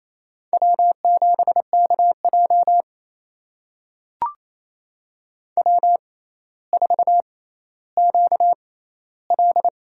Morse code, heard as a keyed tone, about 28 wpm, spaced as for 14 wpm, 700 Hz. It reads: W7KJ W4QL
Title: USA Call Signs - 6x Character Spacing - 28wpm